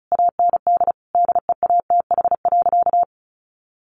Morse code, heard as a keyed tone, about 34 wpm, 700 Hz. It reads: AND DEATH.
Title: Sentences from Top 1100 Words 35wpm